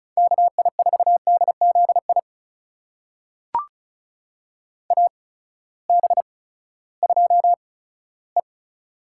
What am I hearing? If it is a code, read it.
Morse code, heard as a keyed tone, about 36 wpm, spaced as for 18 wpm, 700 Hz. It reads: KI4DZI AB2E
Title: USA Call Signs - 8x Character Spacing - 35wpm